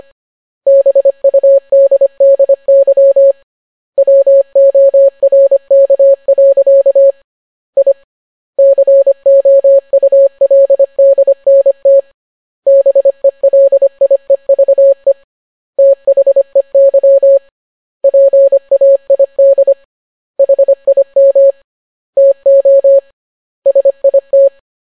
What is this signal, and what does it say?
Morse code, heard as a keyed tone, about 25 wpm, 550 Hz. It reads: BUDDY WORK. I COULDNT BELIEVE THEY PAID HIM TO SIT